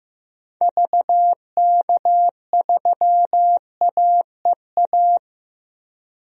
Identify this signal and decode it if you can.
Morse code, heard as a keyed tone, about 15 wpm, 700 Hz. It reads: VK3AEA
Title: DXCC-100 Call Signs - 6 Characters - Repeated 2x - 15wpm